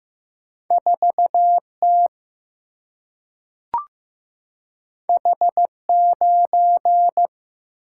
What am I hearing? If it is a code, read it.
Morse code, heard as a keyed tone, about 15 wpm, 700 Hz. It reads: 4T H9
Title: Single-Letter Single-Number 15wpm